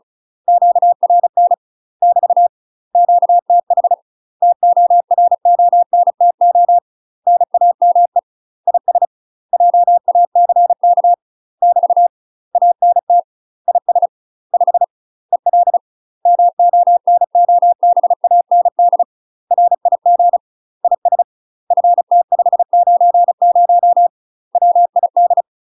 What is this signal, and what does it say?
Morse code, heard as a keyed tone, about 34 wpm, 700 Hz. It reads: QRN <BT> QTH TORONTO NAME IS JACK <BT> ANT IS 5 EL MONOBAND RIG IS FT590 WID
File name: UK MASTER QSOs TRAINING / 35WPM / UK MASTER QSOs 0024